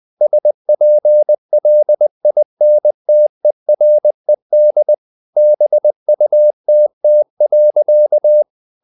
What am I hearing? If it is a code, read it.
Morse code, heard as a keyed tone, about 20 wpm, 600 Hz. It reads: SPLINTERED BUTT.
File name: the old man and the sea part 2 20wpm